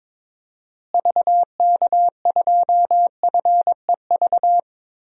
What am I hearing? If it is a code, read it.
Morse code, heard as a keyed tone, about 22 wpm, 700 Hz. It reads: VK2FEV
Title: DXCC-100 Call Signs - 6 Characters - Repeated 2x - 22wpm